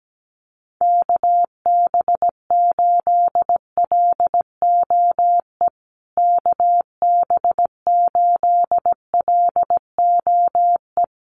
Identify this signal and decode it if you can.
Morse code, heard as a keyed tone, about 17 wpm, 700 Hz. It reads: KB8LOE KB8LOE
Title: USA Call Signs Repeated 2x 17wpm